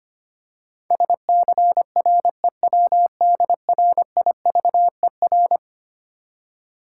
Morse code, heard as a keyed tone, about 25 wpm, 700 Hz. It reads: SCREWDRIVER
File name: practice / 25wpm / Easy Words with Hints - 25wpm